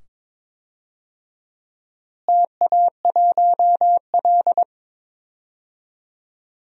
Morse code, heard as a keyed tone, about 22 wpm, 700 Hz. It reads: TA1L
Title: DXCC-100 Call Signs - 4 Characters - Repeated 2x - 22wpm